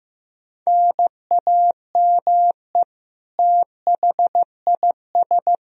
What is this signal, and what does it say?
Morse code, heard as a keyed tone, about 15 wpm, 700 Hz. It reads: NAME THIS
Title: Sentences from Top 1000 Words - Review Part 4 - 15wpm